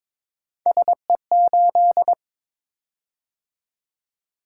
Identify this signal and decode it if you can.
Morse code, heard as a keyed tone, about 22 wpm, 700 Hz. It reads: SE8
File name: Three Letter-Number Combinations - 22wpm